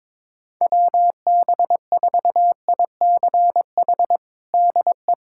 Morse code, heard as a keyed tone, about 22 wpm, 700 Hz. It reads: WB4ICH DE
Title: USA Callsign DE Callsign - 6 Characters - 22wpm